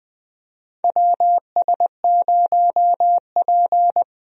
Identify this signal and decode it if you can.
Morse code, heard as a keyed tone, about 20 wpm, 700 Hz. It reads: WS0P
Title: USA Call Signs - 4 Characters - Repeated 3x - 20wpm